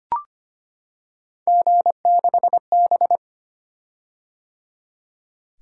Morse code, heard as a keyed tone, about 25 wpm, 700 Hz. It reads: G6B
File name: Three Letter-Number Combinations - 25wpm